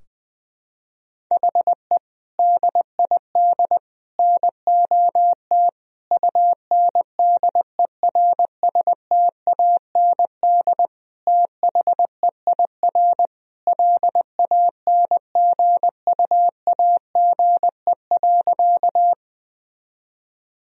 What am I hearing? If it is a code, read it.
Morse code, heard as a keyed tone, about 20 wpm, 700 Hz. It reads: HE DID NOT UNDERSTAND THEIR LANGUAGE.
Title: Sentences from Top 1000 Words - Review Part 3 - 20wpm